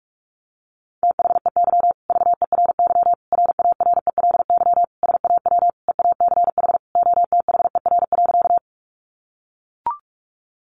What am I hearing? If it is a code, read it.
Morse code, heard as a keyed tone, about 44 wpm, 700 Hz. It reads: THEY VERY RARELY SAW EACH OTHER.
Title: Sentences from Top 1400 Words 45wpm